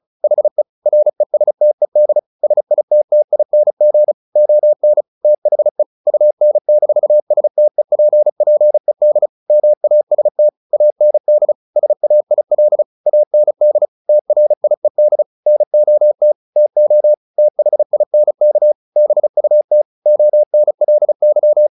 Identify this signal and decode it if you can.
Morse code, heard as a keyed tone, about 34 wpm, 600 Hz. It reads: HE RESTED SITTING ON THE UN-STEPPED MAST AND SAIL AND TRIED NOT TO THINK BUT ONLY
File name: the old man and the sea / the old man and the sea 1 35wpm